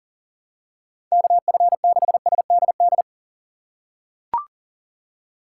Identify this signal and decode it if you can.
Morse code, heard as a keyed tone, about 40 wpm, 700 Hz. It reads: KF6SDD
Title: USA Call Signs - 2x Character Spacing - 40wpm